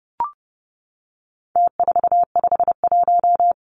Morse code, heard as a keyed tone, about 30 wpm, 700 Hz. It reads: T451
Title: Numbers - Contest Style - 4-Digit 30wpm